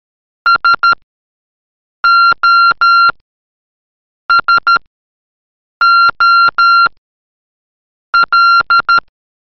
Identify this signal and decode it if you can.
Morse code, heard as a keyed tone, about 13 wpm, spaced as for 6 wpm, 1380 Hz. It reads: SOSOL
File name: vasilkov virt mashina / examen / ЬСОСОЛ МОРЗЕ